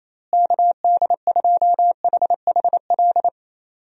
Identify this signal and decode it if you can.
Morse code, heard as a keyed tone, about 28 wpm, 700 Hz. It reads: KD2HHL
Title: USA Call Signs - 6 Characters - Repeated 3x - 28wpm